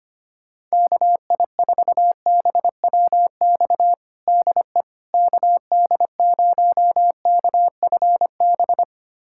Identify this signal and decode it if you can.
Morse code, heard as a keyed tone, about 25 wpm, 700 Hz. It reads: KI4BWX DE KD0KFB
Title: USA Callsign DE Callsign - 6 Characters - 25wpm